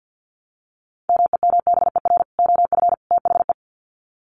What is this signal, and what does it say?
Morse code, heard as a keyed tone, about 48 wpm, 700 Hz. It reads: MEMBER OF THE
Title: Sets of 3 Words 50wpm